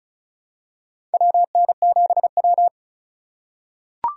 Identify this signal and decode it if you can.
Morse code, heard as a keyed tone, about 36 wpm, 700 Hz. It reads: WN7W
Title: USA Call Signs - 4 Characters - 2x Character Spacing - 35wpm